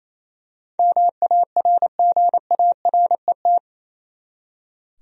Morse code, heard as a keyed tone, about 28 wpm, 700 Hz. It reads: MARGARET
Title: USA Common Names - 28wpm